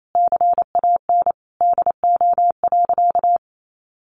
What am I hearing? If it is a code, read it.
Morse code, heard as a keyed tone, about 28 wpm, 700 Hz. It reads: CAN DO.
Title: Sentences from Top 900 Words 28wpm